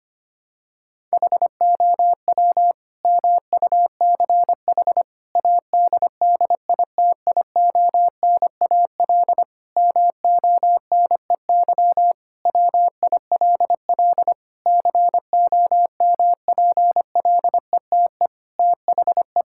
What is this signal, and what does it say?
Morse code, heard as a keyed tone, about 25 wpm, 700 Hz. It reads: HOW MUCH ADDITIONAL MONEY WILL COMPLETE THE
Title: Sentences from Top 1000 Words - Review Part 5 - 25wpm